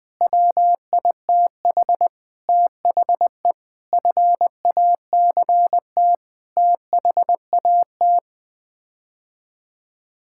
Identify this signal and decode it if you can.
Morse code, heard as a keyed tone, about 20 wpm, 700 Hz. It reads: WITH THE FACT THAT
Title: Sets of 8 Words - 20wpm